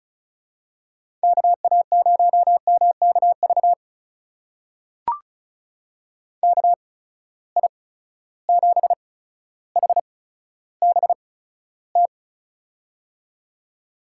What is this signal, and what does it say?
Morse code, heard as a keyed tone, about 36 wpm, spaced as for 15 wpm, 700 Hz. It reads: KA0MKV KI7HBT
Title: USA Call Signs - 6 Characters - 8x Character Spacing - 35wpm